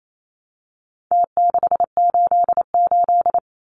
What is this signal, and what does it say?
Morse code, heard as a keyed tone, about 28 wpm, 700 Hz. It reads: T688
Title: Numbers - Contest Style - 4-Digit 28wpm